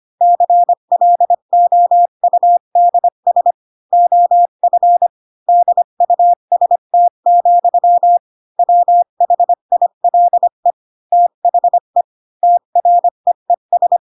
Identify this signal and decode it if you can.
Morse code, heard as a keyed tone, about 25 wpm, 700 Hz. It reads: CLOUDS OF DUST, WHILE THE TREES
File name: Pinocchio / Pinocchio ch6 20 25 ik1hge